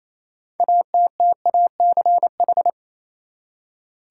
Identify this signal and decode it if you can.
Morse code, heard as a keyed tone, about 28 wpm, 700 Hz. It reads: ATTACH
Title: Six Letter Words - 28wpm